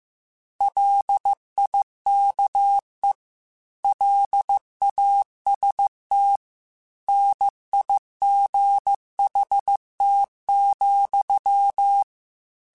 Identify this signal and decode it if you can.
Morse code, heard as a keyed tone, about 15 wpm, 800 Hz. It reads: LIKE LAST NIGHT,
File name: Weather Sets / Weather Sets-mp3s / Weather 4 17 @13